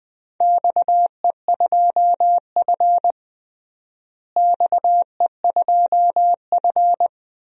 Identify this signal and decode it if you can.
Morse code, heard as a keyed tone, about 20 wpm, 700 Hz. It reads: XE2F XE2F
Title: DXCC-100 Call Signs - 4 Characters - Repeated 3x - 20wpm